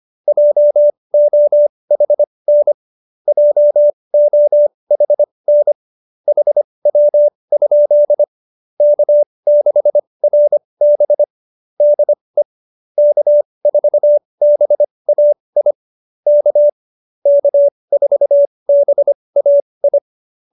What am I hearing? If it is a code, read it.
Morse code, heard as a keyed tone, about 25 wpm, spaced as for 20 wpm, 590 Hz. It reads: JOHN JOHN HW? K6RB DE K4BAI K K4BAI
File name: QSO301 20